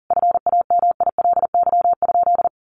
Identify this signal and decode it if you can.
Morse code, heard as a keyed tone, about 40 wpm, 700 Hz. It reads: FAMILY?
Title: Sentences from Top 1300 Words 40wpm